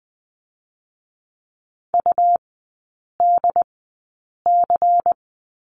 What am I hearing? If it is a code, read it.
Morse code, heard as a keyed tone, about 20 wpm, spaced as for 8 wpm, 700 Hz. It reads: UDC
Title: 152 Lesson 25e - Letter p - Three Characters 20wpm